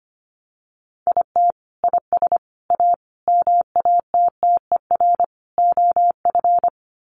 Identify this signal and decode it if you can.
Morse code, heard as a keyed tone, about 25 wpm, 700 Hz. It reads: IT IS A MATTER OF
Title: Sets of 5 Words Encore2 25wpm